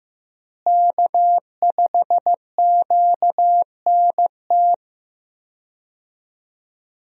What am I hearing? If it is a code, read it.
Morse code, heard as a keyed tone, about 16 wpm, 700 Hz. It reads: K5QNT
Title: USA Call Signs - 5 Characters - Repeated 2x - 15wpm